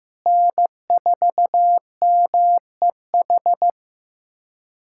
Morse code, heard as a keyed tone, about 15 wpm, 700 Hz. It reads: N4MEH
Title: USA Callsign DE Callsign - 5 Characters - 15wpm